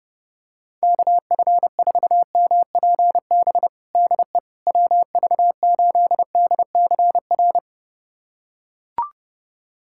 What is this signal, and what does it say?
Morse code, heard as a keyed tone, about 30 wpm, 690 Hz. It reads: KF4MPB DE WV8DCR
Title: USA Callsign DE Callsign 30wpm